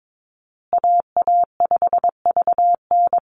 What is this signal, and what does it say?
Morse code, heard as a keyed tone, about 22 wpm, 700 Hz. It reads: AA5VN
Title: USA Call Signs Repeated 3x 22wpm